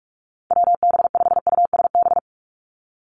Morse code, heard as a keyed tone, about 46 wpm, 700 Hz. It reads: WB5USB
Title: NA SOTA Activator Callsigns 45wpm